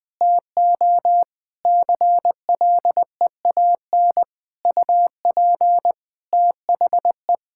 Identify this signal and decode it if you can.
Morse code, heard as a keyed tone, about 21 wpm, 700 Hz. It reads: TO CLEAN UP THE
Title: Difficult Sentences with Hints - 20wpm